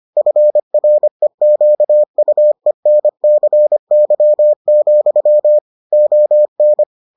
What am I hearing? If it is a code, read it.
Morse code, heard as a keyed tone, about 25 wpm, 600 Hz. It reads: FREQUENCY, ON